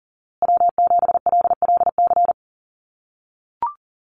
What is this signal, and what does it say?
Morse code, heard as a keyed tone, about 40 wpm, 700 Hz. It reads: W7LLC
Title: USA Call Signs Repeated 2x 40wpm